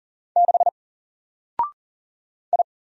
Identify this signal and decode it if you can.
Morse code, heard as a keyed tone, about 40 wpm, 700 Hz. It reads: 6 I
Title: Single Character (Letter:Number:Punctuation:Prosign) Alt 40wpm